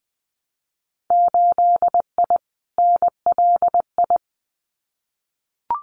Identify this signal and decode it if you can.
Morse code, heard as a keyed tone, about 20 wpm, 700 Hz. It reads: 8I NLI
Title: Winter Field Day Exchanges Encore 20wpm